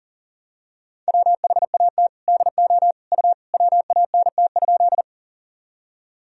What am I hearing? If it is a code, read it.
Morse code, heard as a keyed tone, about 40 wpm, 700 Hz. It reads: WHAT DO U WANT?
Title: Sentences from Top 200 Words and Abbreviations 40wpm